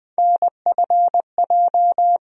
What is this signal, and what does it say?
Morse code, heard as a keyed tone, about 20 wpm, 700 Hz. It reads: NFJ
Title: USA Call Sign Postfixes - 3 Letters 20wpm